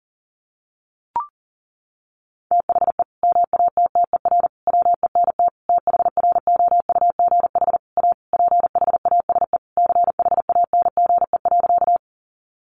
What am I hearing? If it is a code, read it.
Morse code, heard as a keyed tone, about 40 wpm, 700 Hz. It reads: THE MATTER WENT THROUGH A PHASE CHANGE.
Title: Sentences from Top 1200 Words 40wpm